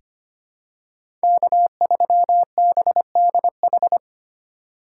Morse code, heard as a keyed tone, about 25 wpm, 700 Hz. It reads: K3BDH
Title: USA Call Signs - 2x Character Spacing - 25wpm